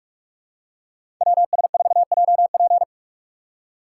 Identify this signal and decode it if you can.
Morse code, heard as a keyed tone, about 46 wpm, 700 Hz. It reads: WS4JP
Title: USA Call Signs - 5 Characters - 2x Character Spacing - 45wpm